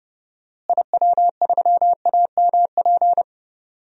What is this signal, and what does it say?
Morse code, heard as a keyed tone, about 29 wpm, 700 Hz. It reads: IW3AMP
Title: DXCC-100 Call Signs - 6 Characters - Repeated 3x - 30wpm